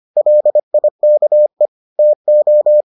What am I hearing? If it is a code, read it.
Morse code, heard as a keyed tone, about 25 wpm, 600 Hz. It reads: LIKE TO